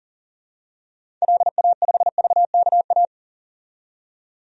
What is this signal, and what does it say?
Morse code, heard as a keyed tone, about 40 wpm, 700 Hz. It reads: LA5VKA
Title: DXCC-100 Call Signs - 6 Characters - 40wpm